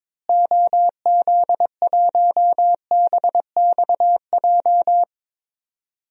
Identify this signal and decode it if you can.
Morse code, heard as a keyed tone, about 22 wpm, 700 Hz. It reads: OZ1BXJ
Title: DXCC-100 Call Signs - 6 Characters - Repeated 3x - 22wpm